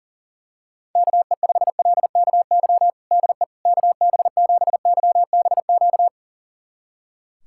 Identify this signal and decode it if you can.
Morse code, heard as a keyed tone, about 40 wpm, 700 Hz. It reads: KE5LKY DE KB7YBQ
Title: USA Callsign DE Callsign - 6 Characters - 40wpm